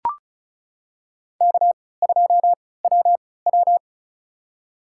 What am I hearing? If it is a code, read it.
Morse code, heard as a keyed tone, about 36 wpm, spaced as for 20 wpm, 700 Hz. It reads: K2WW
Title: USA Call Signs - 4 Characters - 3x Character Spacing - 35wpm